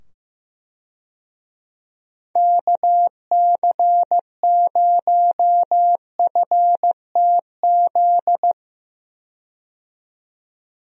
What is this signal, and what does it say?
Morse code, heard as a keyed tone, about 15 wpm, 700 Hz. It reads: KC0FTZ
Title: USA Call Signs - 6 Characters - 15wpm